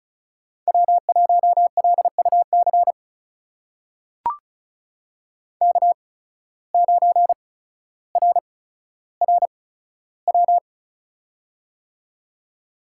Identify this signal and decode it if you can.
Morse code, heard as a keyed tone, about 36 wpm, spaced as for 15 wpm, 700 Hz. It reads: W1LUC K9RRW
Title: USA Call Signs - 5 Characters - 8x Character Spacing - 35wpm